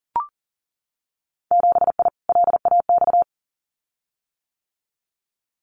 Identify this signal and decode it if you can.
Morse code, heard as a keyed tone, about 40 wpm, 700 Hz. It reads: 7I LAX
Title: Winter Field Day Exchanges Encore 40wpm